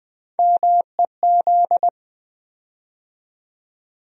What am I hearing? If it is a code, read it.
Morse code, heard as a keyed tone, about 21 wpm, 700 Hz. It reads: MEZ
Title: USA Call Sign Postfixes - 3 Letters 20wpm